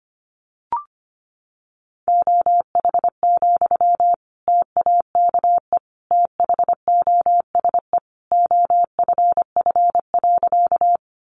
Morse code, heard as a keyed tone, about 25 wpm, 700 Hz. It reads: OH, TAKE THOSE OFF.
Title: Sentences from Top 500 Words 25wpm